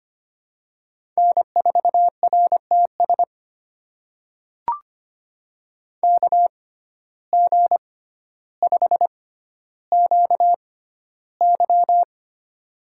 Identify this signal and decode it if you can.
Morse code, heard as a keyed tone, about 25 wpm, spaced as for 13 wpm, 700 Hz. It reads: N4RTS KG5QY
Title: USA Call Signs - 5 Characters - 6x Character Spacing - 25wpm